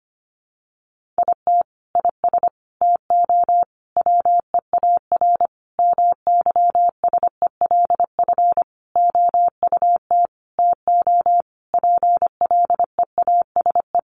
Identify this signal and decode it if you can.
Morse code, heard as a keyed tone, about 25 wpm, 700 Hz. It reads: IT IS TO WEAR MYSELF OUT TO PLEASE
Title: Sentences from Top 1500 Words 25wpm